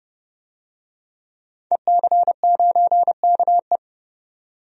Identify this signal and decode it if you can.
Morse code, heard as a keyed tone, about 32 wpm, 700 Hz. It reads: EC9KE
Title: DXCC-100 Call Signs - 5 Characters - 30wpm